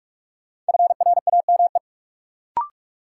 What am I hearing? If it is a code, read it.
Morse code, heard as a keyed tone, about 46 wpm, 700 Hz. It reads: FRAME
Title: Sentences from Top 1000 Words - Review Part 6 - New Word Drill - 45wpm